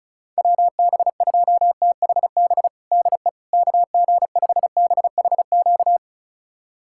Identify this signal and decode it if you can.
Morse code, heard as a keyed tone, about 36 wpm, 690 Hz. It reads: WB2THB DE KG5BHQ
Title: USA Callsign DE Callsign 35wpm